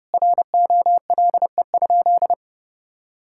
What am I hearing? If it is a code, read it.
Morse code, heard as a keyed tone, about 30 wpm, 700 Hz. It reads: ROLE?
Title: Sentences from Top 1000 Words - Review Part 6 - 30wpm